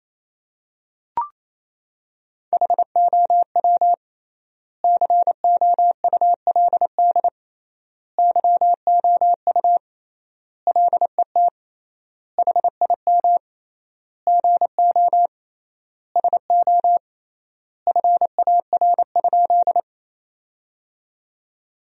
Morse code, heard as a keyed tone, about 27 wpm, 700 Hz. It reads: HOW COULD YOU LET HIM GO SO FAR?
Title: 3x Word-Spacing - Sentences from Top 300 Words - 28wpm